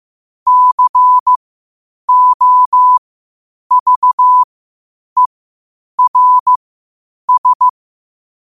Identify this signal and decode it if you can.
Morse code, heard as a keyed tone, about 15 wpm, spaced as for 8 wpm, 1000 Hz. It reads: COVERS
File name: qotdfiles / 8wpm